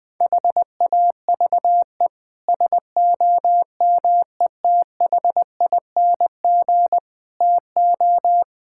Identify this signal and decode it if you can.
Morse code, heard as a keyed tone, about 20 wpm, 700 Hz. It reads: HAVE SOMETHING TO
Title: Sets of 8 Words - 20wpm